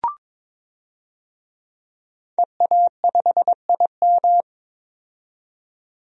Morse code, heard as a keyed tone, about 22 wpm, 700 Hz. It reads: EA5IM
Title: DXCC-100 Call Signs - 5 Characters - 22wpm